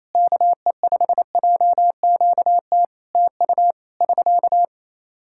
Morse code, heard as a keyed tone, about 28 wpm, 700 Hz. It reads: KE5JQT TU <SK>
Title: Practice QSO - Exchange 4 - Ending - 28wpm